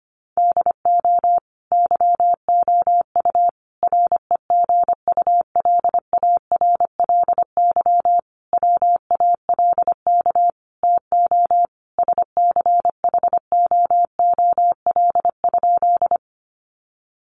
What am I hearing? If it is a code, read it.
Morse code, heard as a keyed tone, about 25 wpm, 700 Hz. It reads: DO YOU REGULARLY WALK TO SCHOOL?
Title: Sentences from Top 1400 Words 25wpm